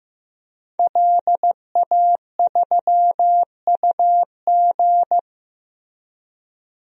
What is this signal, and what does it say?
Morse code, heard as a keyed tone, about 15 wpm, 700 Hz. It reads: LA3UG
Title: DXCC-100 Call Signs - 5 Characters - Repeated 2x - 15wpm